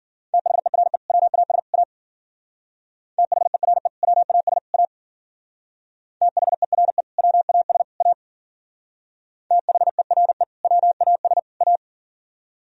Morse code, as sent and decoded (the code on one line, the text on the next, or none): - .... .-. / .-- .- ... / .- / - .... .-. / .-- .- ... / .- / - .... .-. / .-- .- ... / .- / - .... .-. / .-- .- ... / .-
THERE WAS A THERE WAS A THERE WAS A THERE WAS A